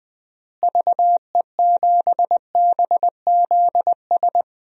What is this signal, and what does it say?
Morse code, heard as a keyed tone, about 20 wpm, 700 Hz. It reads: VE7BZS
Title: DXCC-100 Call Signs - 6 Characters - Repeated 2x - 20wpm